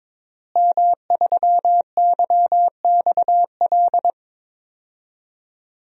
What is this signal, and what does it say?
Morse code, heard as a keyed tone, about 22 wpm, 700 Hz. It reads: M3YXL
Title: DXCC-100 Call Signs - 5 Characters - Repeated 2x - 22wpm